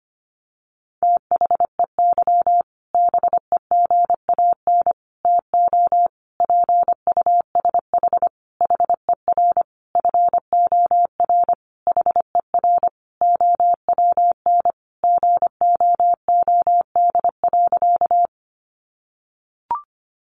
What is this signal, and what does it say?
Morse code, heard as a keyed tone, about 25 wpm, 700 Hz. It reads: THEY BEGAN TO PUSH HER FOR HER OWN GOOD.
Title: Sentences from Top 1300 Words 25wpm